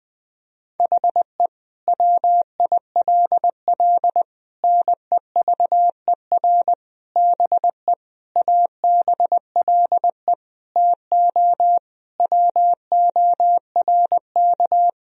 Morse code, heard as a keyed tone, about 19 wpm, 700 Hz. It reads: HE WILL NEVER BE ABLE TO WORK
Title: Sentences from Top 1000 Words - Review Part 2 - 20wpm